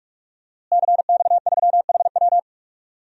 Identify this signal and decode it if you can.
Morse code, heard as a keyed tone, about 46 wpm, 700 Hz. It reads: CX3HW
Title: DXCC-100 Call Signs - 5 Characters - Repeated 3x - 45wpm